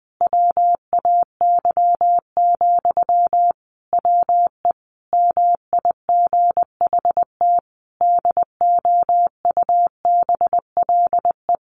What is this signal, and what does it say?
Morse code, heard as a keyed tone, about 19 wpm, 700 Hz. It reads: WAY, WE MIGHT DOUBLE
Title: Sentences from Top 1400 Words 20wpm